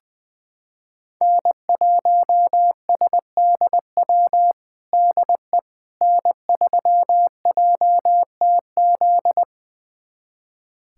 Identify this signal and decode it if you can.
Morse code, heard as a keyed tone, about 20 wpm, 700 Hz. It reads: N1SDW DE N3JTZ
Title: USA Callsign DE Callsign - 5 Characters - 20wpm